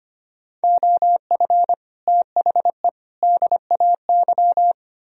Morse code, heard as a keyed tone, about 27 wpm, 700 Hz. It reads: OF THE DAY
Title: Speed-Racing - Sets of 3 Words 25wpm